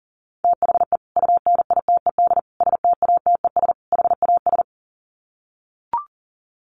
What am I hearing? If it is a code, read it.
Morse code, heard as a keyed tone, about 40 wpm, 700 Hz. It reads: THE UNITED STATES HAS
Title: Sets of 4 Words 40wpm